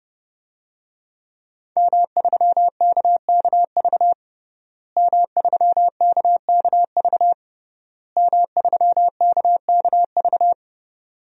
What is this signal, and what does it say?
Morse code, heard as a keyed tone, about 30 wpm, 700 Hz. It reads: M3KKV M3KKV M3KKV
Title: DXCC-100 Call Signs - 5 Characters - Repeated 3x - 30wpm